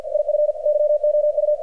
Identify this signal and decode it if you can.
Morse code, heard as a keyed tone, about 48 wpm, 600 Hz, with radio noise. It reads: IF YOU
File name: archived / CRSnet